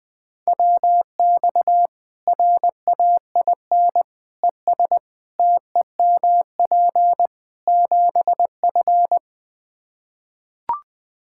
Morse code, heard as a keyed tone, about 20 wpm, 700 Hz. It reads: WX RAIN ES TEMP 7F
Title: Practice QSO - Exchange 2 - Info - 20wpm